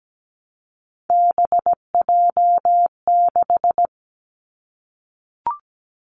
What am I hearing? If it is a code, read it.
Morse code, heard as a keyed tone, about 17 wpm, 700 Hz. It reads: BJ6
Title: Three Challenging Letter-Number Combinations 17wpm